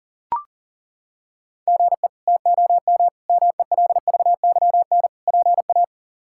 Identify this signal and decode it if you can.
Morse code, heard as a keyed tone, about 37 wpm, 700 Hz. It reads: GE TOM MELVYN PA
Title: K1USN SST QSO Practice - 40wpm